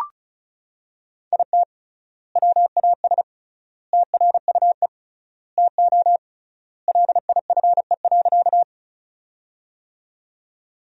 Morse code, heard as a keyed tone, about 36 wpm, 700 Hz. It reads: IT WAS TRUE TO LIFE.
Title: 3x Word-Spacing - Sentences from Top 400 Words - 35wpm